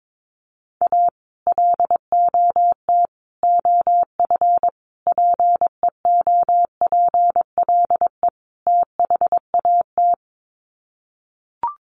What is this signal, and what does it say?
Morse code, heard as a keyed tone, about 22 wpm, 700 Hz. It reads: A LOT OF PEOPLE THAT
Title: Sets of 5 Words Encore2 22wpm